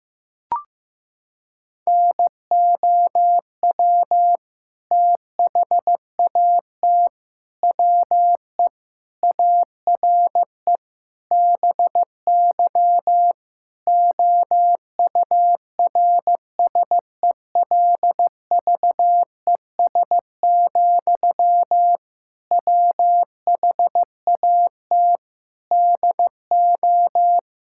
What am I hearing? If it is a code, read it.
Morse code, heard as a keyed tone, about 16 wpm, 700 Hz. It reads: NOW THAT WE ARE BY OURSELVES, WHAT DO
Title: Sentences from Top 1600 Words 15wpm